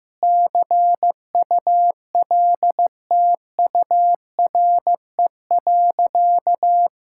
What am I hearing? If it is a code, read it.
Morse code, heard as a keyed tone, about 15 wpm, 700 Hz. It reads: CULTURE.
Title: Sentences from Top 1000 Words - Review Part 6 - 15wpm